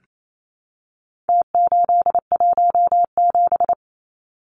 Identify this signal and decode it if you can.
Morse code, heard as a keyed tone, about 28 wpm, 700 Hz. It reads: T817
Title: Numbers - Contest Style - 4-Digit 28wpm